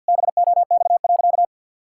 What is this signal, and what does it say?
Morse code, heard as a keyed tone, about 50 wpm, 700 Hz. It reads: BOX.
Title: Pangram Challenge v2 - 50wpm